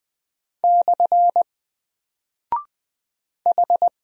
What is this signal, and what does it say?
Morse code, heard as a keyed tone, about 20 wpm, 700 Hz. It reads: / H
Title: Single Character (Letter:Number:Punctuation:Prosign) Alt 20wpm